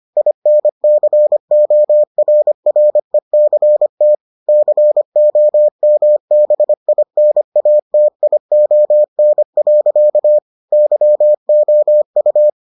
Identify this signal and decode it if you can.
Morse code, heard as a keyed tone, about 25 wpm, 600 Hz. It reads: INCORRECT COMBINATION. YOU